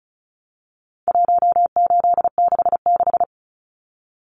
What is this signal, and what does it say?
Morse code, heard as a keyed tone, about 34 wpm, 700 Hz. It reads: 1866
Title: Numbers - 4-Digit 35wpm